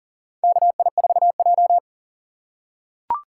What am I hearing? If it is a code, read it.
Morse code, heard as a keyed tone, about 40 wpm, 700 Hz. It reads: KI4J
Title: USA Call Signs - 2x Character Spacing - 40wpm